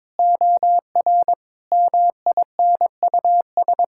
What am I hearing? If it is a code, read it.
Morse code, heard as a keyed tone, about 22 wpm, 700 Hz. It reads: OR MINUS
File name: practice / 22wpm / Sets of 8 Words - 22wpm